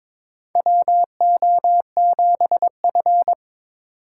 Morse code, heard as a keyed tone, about 22 wpm, 700 Hz. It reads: WO7F
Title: USA Call Signs - 4 Characters - 2x Character Spacing - 22wpm